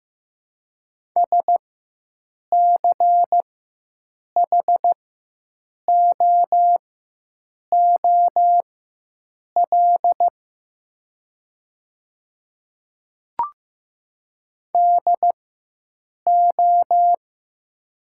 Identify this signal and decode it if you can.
Morse code, heard as a keyed tone, about 15 wpm, spaced as for 7 wpm, 700 Hz. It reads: SCHOOL DO
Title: 4x Character Spacing - Top 300 Words - 15wpm